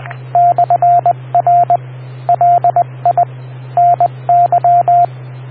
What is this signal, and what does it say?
Morse code, heard as a keyed tone, about 20 wpm, 690 Hz, with radio noise. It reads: /R LI NY